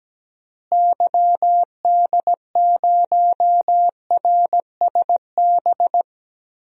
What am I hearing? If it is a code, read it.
Morse code, heard as a keyed tone, about 17 wpm, 700 Hz. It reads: YD0RSB
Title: DXCC-100 Call Signs - 6 Characters - Repeated 3x - 17wpm